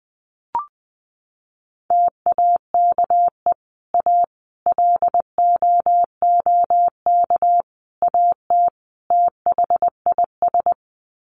Morse code, heard as a keyed tone, about 20 wpm, 700 Hz. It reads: TAKE A LOOK AT THIS
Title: Sets of 5 Words 20wpm